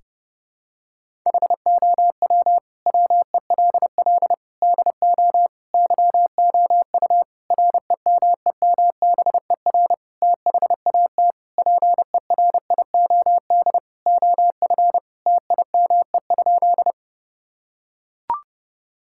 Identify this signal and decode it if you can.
Morse code, heard as a keyed tone, about 32 wpm, 700 Hz. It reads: HOW WELL DO YOU REMEMBER THAT PERIOD OF TIME?
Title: Sentences from Top 1000 Words - Review Part 5 - 30wpm